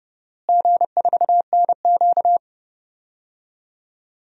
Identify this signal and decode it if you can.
Morse code, heard as a keyed tone, about 31 wpm, 700 Hz. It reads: G4NQ
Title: DXCC-100 Call Signs - 4 Characters - Repeated 2x - 30wpm